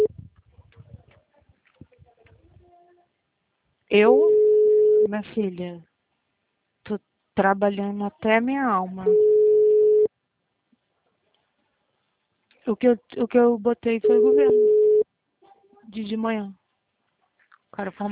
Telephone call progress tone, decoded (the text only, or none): ringback tone